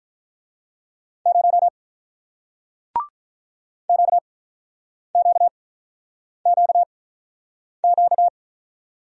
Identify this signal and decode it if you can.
Morse code, heard as a keyed tone, about 48 wpm, spaced as for 9 wpm, 700 Hz. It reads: 0 QQQQ